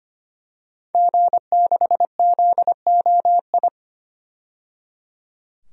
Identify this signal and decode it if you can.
Morse code, heard as a keyed tone, about 25 wpm, 700 Hz. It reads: G6ZOI
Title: DXCC-100 Call Signs - 5 Characters - 25wpm